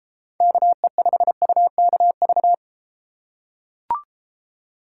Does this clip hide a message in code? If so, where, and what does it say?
Yes: Morse code from 0.4 to 2.5 s, heard as a keyed tone, about 33 wpm, 700 Hz. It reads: KE5UKV